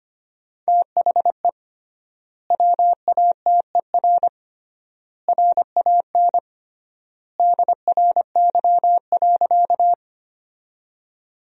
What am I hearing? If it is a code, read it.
Morse code, heard as a keyed tone, about 25 wpm, 700 Hz. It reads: THE WATER RAN DRY.
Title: 3x Word-Spacing - Sentences from Top 500 Words - 25wpm